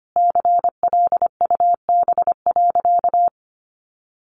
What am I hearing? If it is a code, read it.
Morse code, heard as a keyed tone, about 25 wpm, 700 Hz. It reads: CLUB.
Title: Sentences from Top 1200 Words 25wpm